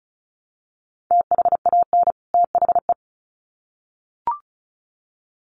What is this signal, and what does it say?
Morse code, heard as a keyed tone, about 34 wpm, 700 Hz. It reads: THAN THE
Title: Sets of 2 Words 35wpm